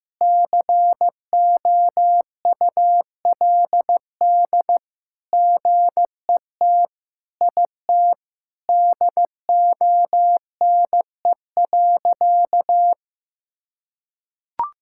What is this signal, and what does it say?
Morse code, heard as a keyed tone, about 15 wpm, 700 Hz. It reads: COULD GET IT DONE.
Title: Sentences from Top 1000 Words - Review Part 5 - 15wpm